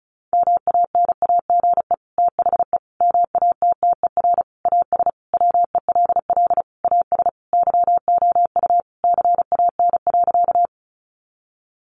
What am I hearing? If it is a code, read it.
Morse code, heard as a keyed tone, about 34 wpm, 700 Hz. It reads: MANAGE THE MATTER AS WELL AS YOU CAN.
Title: Sentences from Top 1400 Words 35wpm